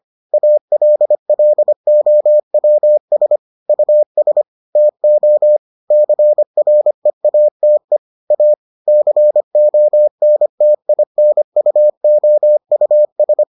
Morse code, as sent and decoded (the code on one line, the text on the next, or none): .- .-.. .-.. --- .-- ... / ..- ... / - --- / -.-. .-. .- - . / .- / -.-. --- -. - .. -. ..- --- ..- ...
ALLOWS US TO CREATE A CONTINUOUS